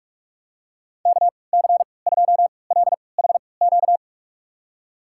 Morse code, heard as a keyed tone, about 46 wpm, spaced as for 25 wpm, 700 Hz. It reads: KC2LHQ
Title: USA Call Signs - 6 Characters - 3x Character Spacing - 45wpm